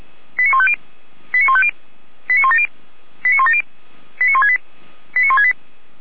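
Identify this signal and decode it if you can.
Five-tone selective call (ZVEI), heard as quick runs of five tones, tones of about 70 ms. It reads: 89150 89150 89170 89170 89168 89168
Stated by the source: ZVEI five-tone call received by SDR.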